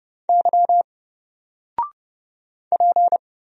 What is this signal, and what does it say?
Morse code, heard as a keyed tone, about 30 wpm, 700 Hz. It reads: Y P
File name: Single-Letter-Number-30wpm